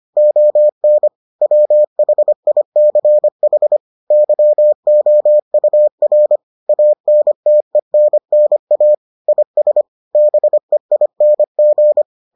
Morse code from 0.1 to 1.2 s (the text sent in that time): ON